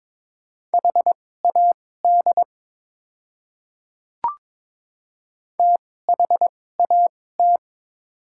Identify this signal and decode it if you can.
Morse code, heard as a keyed tone, about 22 wpm, spaced as for 16 wpm, 700 Hz. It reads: HAD THAT
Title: 2x Character Spacing - Top 200 Words - 22wpm